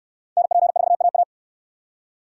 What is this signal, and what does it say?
Morse code, heard as a keyed tone, about 58 wpm, 700 Hz. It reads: NP4AA